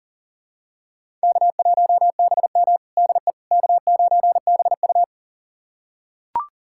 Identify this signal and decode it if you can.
Morse code, heard as a keyed tone, about 37 wpm, 700 Hz. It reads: K1BM DE K9BU
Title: USA Callsign DE Callsign - 4 Characters - 40wpm